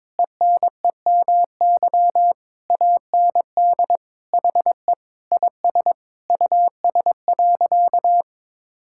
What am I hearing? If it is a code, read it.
Morse code, heard as a keyed tone, about 22 wpm, 700 Hz. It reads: ENEMY AND HE IS US.
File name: Sentences with 9-Word Phrases - 22wpm